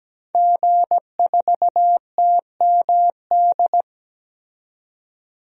Morse code, heard as a keyed tone, about 17 wpm, 700 Hz. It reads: G4TMD
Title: DXCC-100 Call Signs - 5 Characters - 17wpm